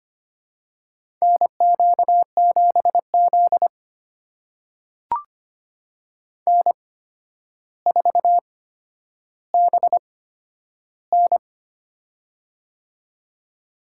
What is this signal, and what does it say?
Morse code, heard as a keyed tone, about 25 wpm, spaced as for 11 wpm, 700 Hz. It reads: NQ7Z N4BN
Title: USA Call Signs - 4 Characters - 8x Character Spacing - 25wpm